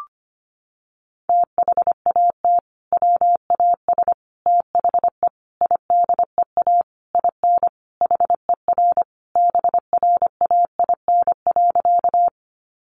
Morse code, heard as a keyed tone, about 25 wpm, 700 Hz. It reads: THAT WAS THE IDEA IN HER BRAIN.
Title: Sentences from Top 1100 Words 25wpm